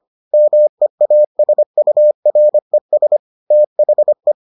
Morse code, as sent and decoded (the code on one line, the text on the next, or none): -- . .- ... ..- .-. ... / - .... .
MEASURES THE